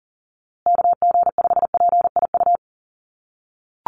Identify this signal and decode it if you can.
Morse code, heard as a keyed tone, about 40 wpm, 700 Hz. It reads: KG5PIU
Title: USA Call Signs Repeated 3x 40wpm